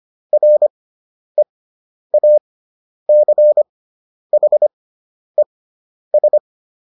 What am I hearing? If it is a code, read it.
Morse code, heard as a keyed tone, about 25 wpm, spaced as for 10 wpm, 600 Hz. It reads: REACHES